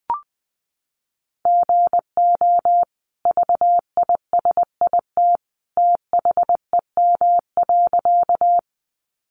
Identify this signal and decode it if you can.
Morse code, heard as a keyed tone, about 20 wpm, 700 Hz. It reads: GO VISIT THEM.
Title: Sentences from Top 900 Words 20wpm